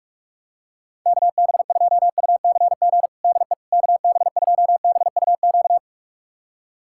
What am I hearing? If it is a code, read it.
Morse code, heard as a keyed tone, about 45 wpm, 700 Hz. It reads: KB2UCG DE KB2BUQ
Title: USA Callsign DE Callsign - 6 Characters - 45wpm